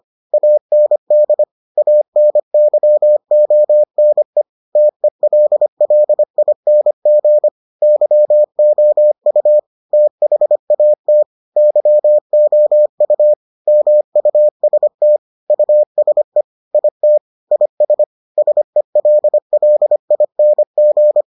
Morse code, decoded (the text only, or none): AND ANYONE TELLING YOU THAT YOU MUST USE IT IS SELLING